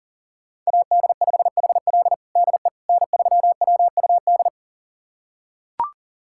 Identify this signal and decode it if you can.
Morse code, heard as a keyed tone, about 40 wpm, 700 Hz. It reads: AD5HL DE N3WUD
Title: USA Callsign DE Callsign - 5 Characters - 40wpm